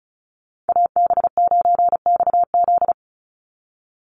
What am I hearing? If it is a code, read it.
Morse code, heard as a keyed tone, about 34 wpm, 700 Hz. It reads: AB9XZ
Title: USA Call Signs 35wpm